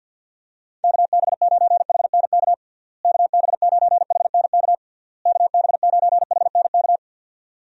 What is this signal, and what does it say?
Morse code, heard as a keyed tone, about 52 wpm, 700 Hz. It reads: KB9HNK KB9HNK KB9HNK
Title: USA Call Signs - 6 Characters - Repeated 3x - 50wpm